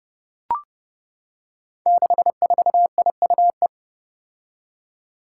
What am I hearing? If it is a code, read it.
Morse code, heard as a keyed tone, about 30 wpm, 700 Hz. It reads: 64IUE
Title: Five Challenging Letter-Number Combinations 30wpm